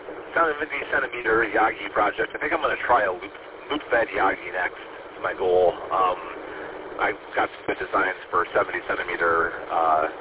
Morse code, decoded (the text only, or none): none